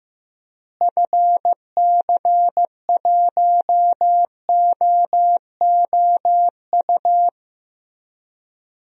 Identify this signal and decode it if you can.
Morse code, heard as a keyed tone, about 16 wpm, 700 Hz. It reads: FC1OOU
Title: DXCC-100 Call Signs - 6 Characters - Repeated 3x - 15wpm